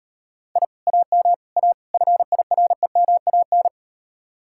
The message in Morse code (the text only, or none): I AM A FIREMAN